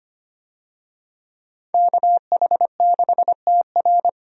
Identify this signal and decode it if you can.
Morse code, heard as a keyed tone, about 25 wpm, 700 Hz. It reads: KH6TR
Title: DXCC-100 Call Signs - 5 Characters - 25wpm